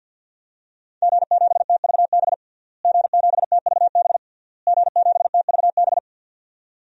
Morse code, heard as a keyed tone, about 53 wpm, 700 Hz. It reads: G7TVB G7TVB G7TVB
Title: DXCC-100 Call Signs - 5 Characters - Repeated 3x - 50wpm